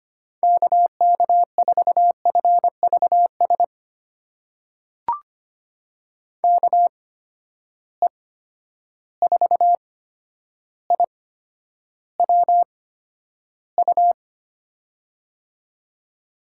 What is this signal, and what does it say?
Morse code, heard as a keyed tone, about 25 wpm, spaced as for 11 wpm, 700 Hz. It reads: KK4FVS KE4IWU
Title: USA Call Signs - 6 Characters - 8x Character Spacing - 25wpm